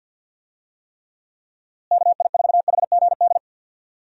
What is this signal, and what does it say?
Morse code, heard as a keyed tone, about 49 wpm, 700 Hz. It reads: KI4HGD
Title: USA Call Signs - 6 Characters - Repeated 2x - 50wpm